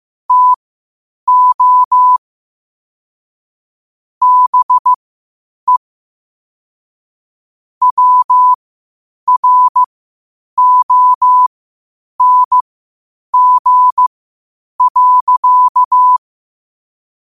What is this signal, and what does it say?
Morse code, heard as a keyed tone, about 15 wpm, spaced as for 8 wpm, 1000 Hz. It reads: TO BE WRONG.